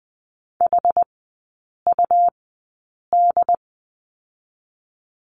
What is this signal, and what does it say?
Morse code, heard as a keyed tone, about 20 wpm, 700 Hz. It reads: H U D
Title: 114 Lesson 19e - Letter w - Three Characters 20wpm